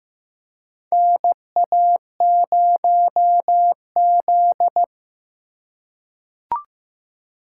Morse code, heard as a keyed tone, about 15 wpm, 700 Hz. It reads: NA0Z
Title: USA Call Signs - 4 Characters - 2x Character Spacing - 15wpm